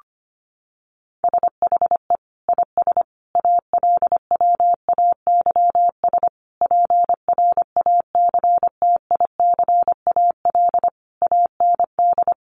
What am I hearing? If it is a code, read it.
Morse code, heard as a keyed tone, about 25 wpm, 700 Hz. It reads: SHE IS ALWAYS PRACTICAL AND
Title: Sentences from Top 1400 Words 25wpm